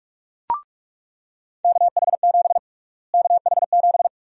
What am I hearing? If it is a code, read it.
Morse code, heard as a keyed tone, about 46 wpm, 700 Hz. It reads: KH7 KH7
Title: USA Call Sign Prefixes - Repeated 2x - 45wpm